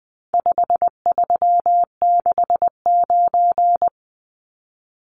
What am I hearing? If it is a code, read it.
Morse code, heard as a keyed tone, about 20 wpm, 700 Hz. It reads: 5369
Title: Numbers - 4-Digit 20wpm